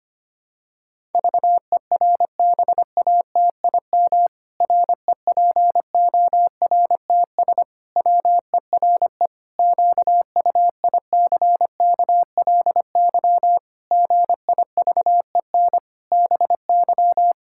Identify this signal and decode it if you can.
Morse code, heard as a keyed tone, about 25 wpm, 700 Hz. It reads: VERBATIM REPORTS WERE QUICKLY GIVEN BY
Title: Pangram Challenge v2 - 25wpm